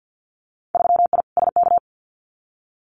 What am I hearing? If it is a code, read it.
Morse code, heard as a keyed tone, about 50 wpm, 700 Hz. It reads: 3I SK
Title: Winter Field Day Exchanges Encore 50wpm